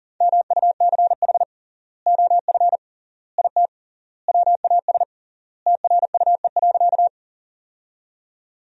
Morse code, heard as a keyed tone, about 40 wpm, 700 Hz. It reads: MUCH OF IT WAS TRUE.
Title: 3x Word-Spacing - Sentences from Top 400 Words - 40wpm